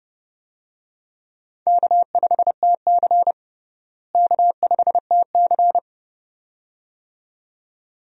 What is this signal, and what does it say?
Morse code, heard as a keyed tone, about 30 wpm, 700 Hz. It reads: K5TC K5TC
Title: USA Call Signs - 4 Characters - Repeated 2x - 30wpm